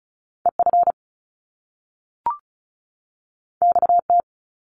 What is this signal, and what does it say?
Morse code, heard as a keyed tone, about 36 wpm, 700 Hz. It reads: EF XT
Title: Top Two Letter Combinations in Words 35wpm